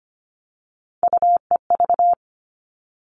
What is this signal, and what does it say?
Morse code, heard as a keyed tone, about 25 wpm, 700 Hz. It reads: UEV
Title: All Cut Numbers - 3-Digit 25wpm